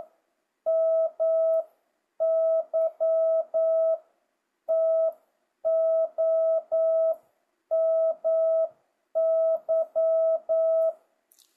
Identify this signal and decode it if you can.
Morse code, heard as a keyed tone, about 9 wpm, spaced as for 8 wpm, 650 Hz. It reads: MY TO MY
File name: jh8cbh / 2025cw